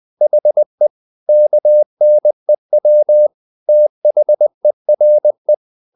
Morse code, heard as a keyed tone, about 20 wpm, 600 Hz. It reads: HE KNEW THERE
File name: the old man and the sea part 2 20wpm